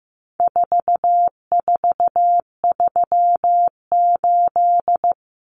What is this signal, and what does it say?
Morse code, heard as a keyed tone, about 15 wpm, 700 Hz. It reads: 4438
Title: Numbers - 4-Digit 15wpm